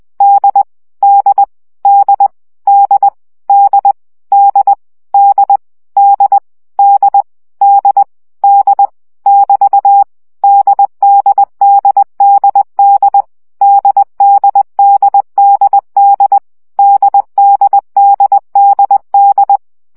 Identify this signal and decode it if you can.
Morse code, heard as a keyed tone, about 20 wpm, 800 Hz. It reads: D D D D D D D D D D D <BT> DDDDD DDDDD DDDDD
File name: CW gyakorlas d 17 baud